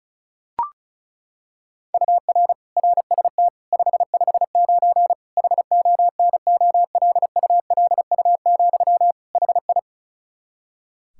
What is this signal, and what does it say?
Morse code, heard as a keyed tone, about 36 wpm, 700 Hz. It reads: UR RST 559 HONOLULU, HI
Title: Practice QSO - Exchange 1 - Beginning - 35wpm